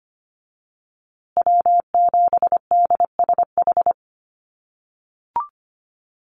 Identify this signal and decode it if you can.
Morse code, heard as a keyed tone, about 25 wpm, 700 Hz. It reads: W7DSH
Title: USA Call Signs Repeated 3x 25wpm